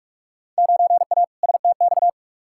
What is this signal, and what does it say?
Morse code, heard as a keyed tone, about 42 wpm, 700 Hz. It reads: 9A STX
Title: Field Day Exchanges - 45wpm